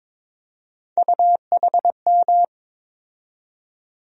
Morse code, heard as a keyed tone, about 22 wpm, 700 Hz. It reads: UHM
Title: USA Call Sign Postfixes - 3 Letters 22wpm